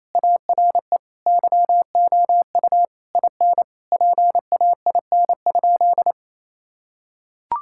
Morse code, heard as a keyed tone, about 28 wpm, 700 Hz. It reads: ARE YOU IN PAIN?
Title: Sentences from Top 1600 Words 28wpm